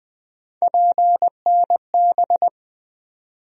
Morse code, heard as a keyed tone, about 20 wpm, 700 Hz. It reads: PNB
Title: USA Call Sign Postfixes - 3 Letters 20wpm